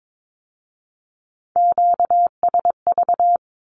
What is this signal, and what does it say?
Morse code, heard as a keyed tone, about 22 wpm, 700 Hz. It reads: QSV
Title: Q-Codes with Spoken Meaning - 22wpm